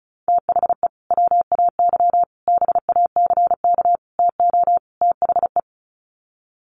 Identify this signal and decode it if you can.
Morse code, heard as a keyed tone, about 34 wpm, 700 Hz. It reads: THE WAY BACK TO THE
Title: Sets of 5 Words Encore2 35wpm